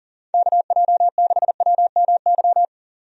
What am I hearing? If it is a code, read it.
Morse code, heard as a keyed tone, about 40 wpm, 700 Hz. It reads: KJ6WMY
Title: USA Call Signs - 6 Characters - 40wpm